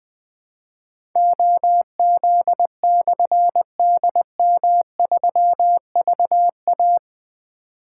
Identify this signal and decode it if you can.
Morse code, heard as a keyed tone, about 21 wpm, 700 Hz. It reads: OZ/DM3VA
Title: Difficult Call Signs 20wpm